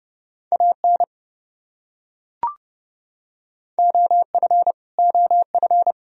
Morse code, heard as a keyed tone, about 32 wpm, 700 Hz. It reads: AN OF OF
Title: Top 100 Words Repeated 30wpm